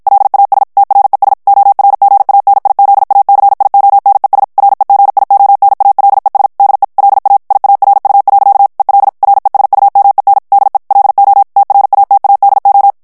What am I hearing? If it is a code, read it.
Morse code, heard as a keyed tone, about 55 wpm, 800 Hz. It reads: LAS TRES ORGANIZACIONES REGIONALES DE LA IARU. EL RESUMEN DE LO TRATADO
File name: ESPAGNOL TEXTE EN CLAIR 1 VITESSE 50 MOTS MINUTES